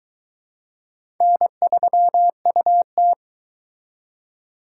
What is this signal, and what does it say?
Morse code, heard as a keyed tone, about 22 wpm, 700 Hz. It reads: N3UT